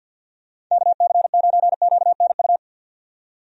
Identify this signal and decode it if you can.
Morse code, heard as a keyed tone, about 52 wpm, 700 Hz. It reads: KC9QNU
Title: USA Call Signs - 6 Characters - 50wpm